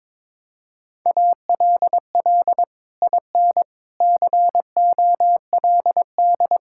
Morse code, heard as a keyed tone, about 22 wpm, 700 Hz. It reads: ALL IN COLD
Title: Sentences from Top 1000 Words - Review Part 5 - 22wpm